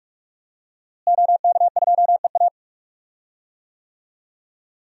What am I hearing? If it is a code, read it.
Morse code, heard as a keyed tone, about 42 wpm, 700 Hz. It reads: OK2EA
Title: DXCC-100 Call Signs - 5 Characters - Repeated 2x - 45wpm